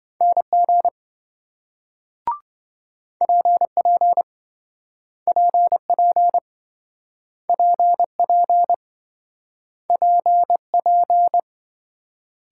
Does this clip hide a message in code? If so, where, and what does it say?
Yes: Morse code from 0.2 to 11.4 s, heard as a keyed tone, about 28 wpm, 700 Hz. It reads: NG PP PP PP PP